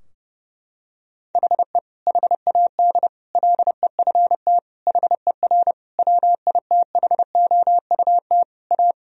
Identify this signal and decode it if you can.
Morse code, heard as a keyed tone, about 30 wpm, 700 Hz. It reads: HE HAD LEFT HER WITHOUT A
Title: Sentences from Top 1000 Words - Review Part 3 - 30wpm